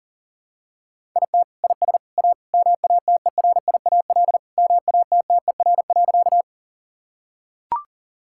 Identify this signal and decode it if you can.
Morse code, heard as a keyed tone, about 40 wpm, 700 Hz. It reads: IT IS A MATERIAL MATTER.
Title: Sentences from Top 1000 Words - Review Part 5 - 40wpm